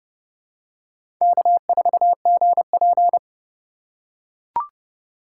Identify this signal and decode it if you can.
Morse code, heard as a keyed tone, about 30 wpm, 700 Hz. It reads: K4GP
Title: USA Call Signs - 4 Characters - 4x Character Spacing - 30wpm